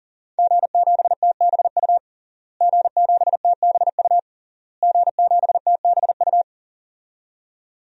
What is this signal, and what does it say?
Morse code, heard as a keyed tone, about 40 wpm, 700 Hz. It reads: G7TBU G7TBU G7TBU
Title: DXCC-100 Call Signs - 5 Characters - Repeated 3x - 40wpm